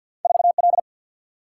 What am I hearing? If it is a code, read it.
Morse code, heard as a keyed tone, about 51 wpm, 700 Hz. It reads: 4L
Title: Single-Letter Single-Number 50wpm